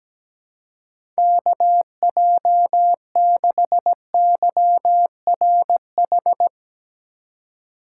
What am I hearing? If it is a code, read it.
Morse code, heard as a keyed tone, about 17 wpm, 700 Hz. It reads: KJ6YRH
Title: USA Call Signs - 6 Characters - 3x Character Spacing - 17wpm